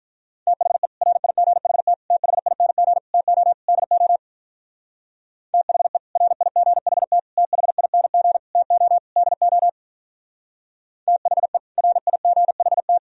Morse code, heard as a keyed tone, about 43 wpm, 700 Hz. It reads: THE RIGHT THING TO DO THE RIGHT THING TO DO THE RIGHT
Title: Speed-Racing - Sets of 5 Words 35wpm